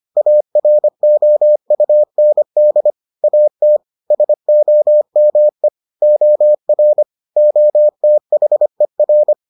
Morse code, heard as a keyed tone, about 25 wpm, 600 Hz. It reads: AROUND AT SOME OR OTHER